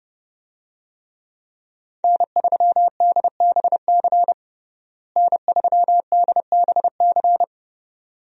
Morse code, heard as a keyed tone, about 30 wpm, 700 Hz. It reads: N3DBC N3DBC
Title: USA Call Signs - 5 Characters - Repeated 2x - 30wpm